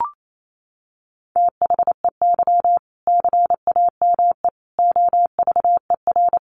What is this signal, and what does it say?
Morse code, heard as a keyed tone, about 28 wpm, 700 Hz. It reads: THEY CAME OVER
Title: Sentences from Top 300 Words 28wpm